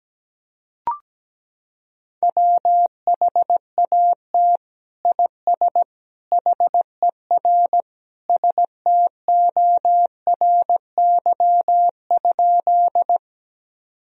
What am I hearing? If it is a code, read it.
Morse code, heard as a keyed tone, about 17 wpm, 700 Hz. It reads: WHAT IS HER STORY?
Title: Sentences from Top 1000 Words - Review Part 1 - 17wpm